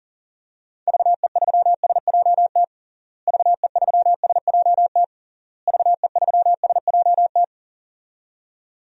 Morse code, heard as a keyed tone, about 40 wpm, 700 Hz. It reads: VE3SJT VE3SJT VE3SJT
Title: DXCC-100 Call Signs - 6 Characters - Repeated 3x - 40wpm